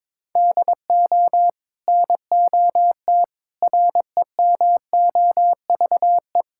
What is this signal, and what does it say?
Morse code, heard as a keyed tone, about 22 wpm, 700 Hz. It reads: DO NOT REMOVE
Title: Sentences from Top 1000 Words - Review Part 5 - 22wpm